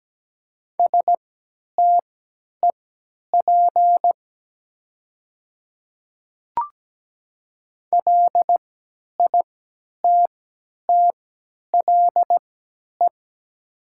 Morse code, heard as a keyed tone, about 17 wpm, spaced as for 10 wpm, 700 Hz. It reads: STEP LITTLE
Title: 3x Character Spacing - Top 500 Words - 17wpm